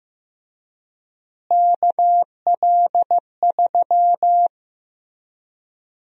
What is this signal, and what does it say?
Morse code, heard as a keyed tone, about 16 wpm, 700 Hz. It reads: KL3
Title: USA Call Sign Prefixes - Repeated 2x - 15wpm